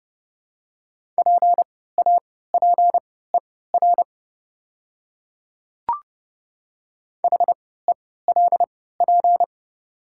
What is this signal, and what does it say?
Morse code, heard as a keyed tone, about 30 wpm, spaced as for 17 wpm, 700 Hz. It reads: PAPER HELP
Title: 3x Character Spacing - Top 500 Words - 30wpm